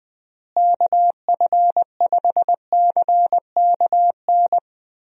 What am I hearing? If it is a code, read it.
Morse code, heard as a keyed tone, about 20 wpm, 700 Hz. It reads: KF5CKN
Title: USA Call Signs - 6 Characters - Repeated 2x - 20wpm